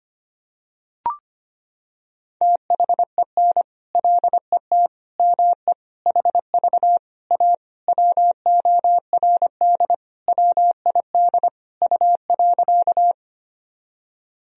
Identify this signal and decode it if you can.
Morse code, heard as a keyed tone, about 25 wpm, 700 Hz. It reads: THEN LET ME HV A WORD WID U.
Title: Sentences from Top 300 Words and Abbreviations 25wpm